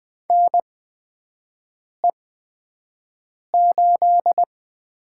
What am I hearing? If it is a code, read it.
Morse code, heard as a keyed tone, about 20 wpm, spaced as for 5 wpm, 700 Hz. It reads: NE8
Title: USA Call Signs - 8x Character Spacing - 20wpm